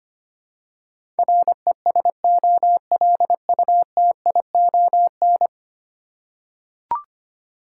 Morse code, heard as a keyed tone, about 25 wpm, 700 Hz. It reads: RESOLUTION
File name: Ten Letter Words - 25wpm